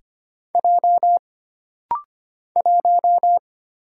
Morse code, heard as a keyed tone, about 25 wpm, 700 Hz. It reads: J 1
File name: Single Character (Letter:Number:Punctuation:Prosign) - Rapid-Fire - 25wpm